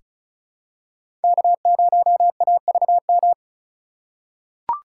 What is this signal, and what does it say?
Morse code, heard as a keyed tone, about 36 wpm, 700 Hz. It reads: K0AVM
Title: USA Call Signs - 5 Characters - 2x Character Spacing - 35wpm